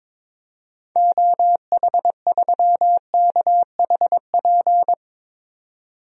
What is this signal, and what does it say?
Morse code, heard as a keyed tone, about 22 wpm, 700 Hz. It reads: OH3KHP
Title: DXCC-100 Call Signs - 6 Characters - 22wpm